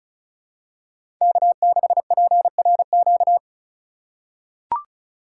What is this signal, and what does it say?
Morse code, heard as a keyed tone, about 36 wpm, 700 Hz. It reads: K6PRQ
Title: USA Call Signs - 5 Characters - 3x Character Spacing - 35wpm